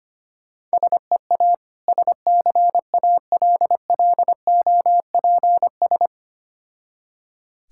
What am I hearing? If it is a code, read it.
Morse code, heard as a keyed tone, about 25 wpm, 700 Hz. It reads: SEA SCALLOPS
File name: practice / 25wpm / Difficult Words with Hints - 25wpm